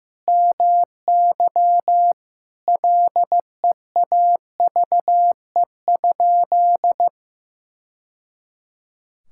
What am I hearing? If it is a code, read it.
Morse code, heard as a keyed tone, about 15 wpm, 700 Hz. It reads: MY LEAVE?
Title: Sentences from Top 1000 Words - Review Part 5 - 15wpm